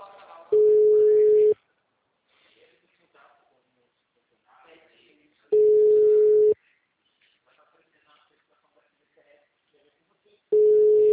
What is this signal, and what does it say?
Ringback tone (Europe, 425 Hz), with speech.
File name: out- -2030- - - .215